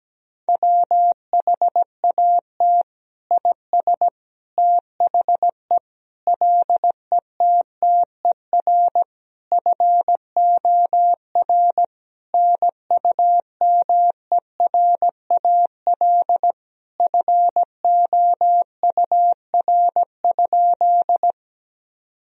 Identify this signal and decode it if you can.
Morse code, heard as a keyed tone, about 17 wpm, 700 Hz. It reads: WHAT IS THE LETTER FOR NUMERAL FOUR?
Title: Sentences from Top 1000 Words - Review Part 2 - 17wpm